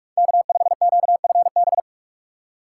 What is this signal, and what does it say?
Morse code, heard as a keyed tone, about 45 wpm, 700 Hz. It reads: K5QFB